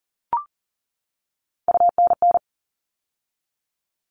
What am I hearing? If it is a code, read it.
Morse code, heard as a keyed tone, about 40 wpm, 700 Hz. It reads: UNN
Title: All Cut Numbers - 3-Digit 40wpm